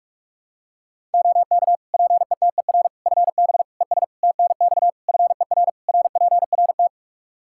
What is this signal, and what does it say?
Morse code, heard as a keyed tone, about 42 wpm, 700 Hz. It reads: OK PETER FB ES TNX FER RPRT
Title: K1USN SST QSO Practice 45wpm